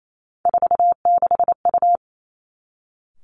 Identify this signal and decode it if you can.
Morse code, heard as a keyed tone, about 28 wpm, 700 Hz. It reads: 46U
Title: Three Challenging Letter-Number Combinations 28wpm